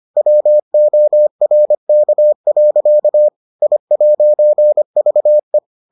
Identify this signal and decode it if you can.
Morse code, heard as a keyed tone, about 25 wpm, 600 Hz. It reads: WORK. I'VE